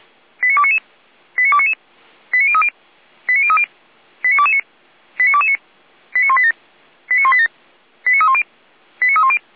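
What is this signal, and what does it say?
Five-tone selective call (ZVEI), heard as quick runs of five tones, tones of about 70 ms. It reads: 89299 89299 89030 89030 89229 89229 89187 89187 89310 89310
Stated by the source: ZVEI five-tone selective call 89240 received on SDR, from Feuerwehr Orlamünde.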